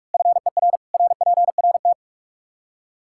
Morse code, heard as a keyed tone, about 46 wpm, 700 Hz. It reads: FER RPRT
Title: Practice QSO - Exchange 2 - Info - 45wpm